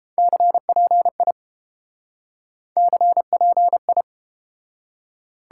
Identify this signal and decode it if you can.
Morse code, heard as a keyed tone, about 31 wpm, 700 Hz. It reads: CPI CPI